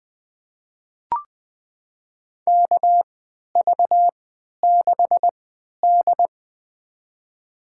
Morse code, heard as a keyed tone, about 20 wpm, spaced as for 11 wpm, 700 Hz. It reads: KV6D
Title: USA Call Signs - 3x Character Spacing - 20wpm